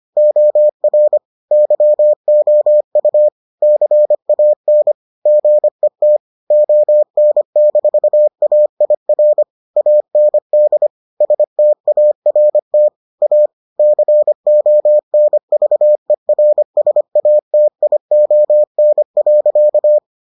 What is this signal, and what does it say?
Morse code, heard as a keyed tone, about 25 wpm, 600 Hz. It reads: OR YOU CAN GET ON-AIR AND START A CONVERSATION.